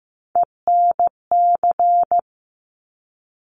Morse code, heard as a keyed tone, about 15 wpm, 700 Hz. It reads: ENC
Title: Top Three Letter Combinations in Words 15wpm